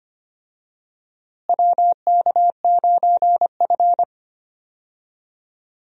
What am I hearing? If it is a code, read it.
Morse code, heard as a keyed tone, about 25 wpm, 700 Hz. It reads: WK9F
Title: USA Call Signs - 4 Characters - Repeated 3x - 25wpm